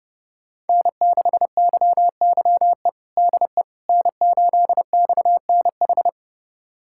Morse code, heard as a keyed tone, about 30 wpm, 700 Hz. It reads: N6YYE DE N8XNH
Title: USA Callsign DE Callsign - 5 Characters - 30wpm